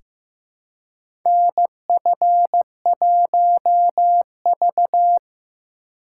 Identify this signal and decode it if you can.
Morse code, heard as a keyed tone, about 15 wpm, 700 Hz. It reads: NF1V
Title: USA Call Signs - 4 Characters - 3x Character Spacing - 15wpm